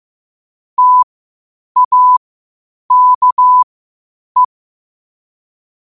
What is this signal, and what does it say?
Morse code, heard as a keyed tone, about 15 wpm, spaced as for 8 wpm, 1000 Hz. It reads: TAKE
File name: qotdfiles / 8wpm